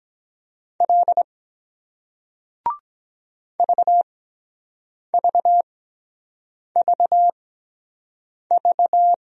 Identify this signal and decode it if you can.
Morse code, heard as a keyed tone, about 23 wpm, spaced as for 7 wpm, 700 Hz. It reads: L VVVV